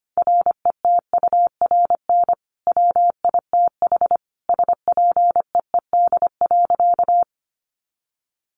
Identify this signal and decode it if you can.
Morse code, heard as a keyed tone, about 25 wpm, 700 Hz. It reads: RETURN WITH SPEED.
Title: Sentences from Top 800 Words 25wpm